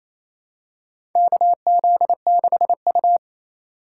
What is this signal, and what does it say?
Morse code, heard as a keyed tone, about 28 wpm, 700 Hz. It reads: KZ6U
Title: USA Call Signs - 4x Character Spacing - 28wpm